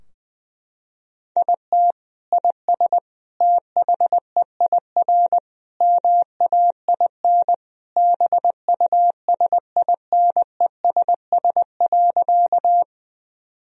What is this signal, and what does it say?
Morse code, heard as a keyed tone, about 21 wpm, 700 Hz. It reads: IT IS THEIR MAIN BUSINESS.
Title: Sentences from Top 1000 Words - Review Part 2 - 20wpm